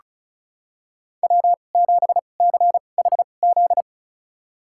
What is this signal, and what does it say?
Morse code, heard as a keyed tone, about 36 wpm, spaced as for 25 wpm, 700 Hz. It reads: W7CHZ
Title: USA Call Signs - 5 Characters - 2x Character Spacing - 35wpm